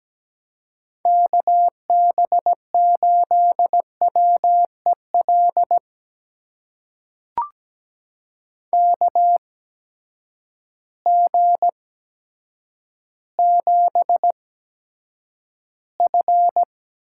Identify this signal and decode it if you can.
Morse code, heard as a keyed tone, about 17 wpm, spaced as for 8 wpm, 700 Hz. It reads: KB8WEL KG7F
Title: USA Call Signs - 6 Characters - 8x Character Spacing - 17wpm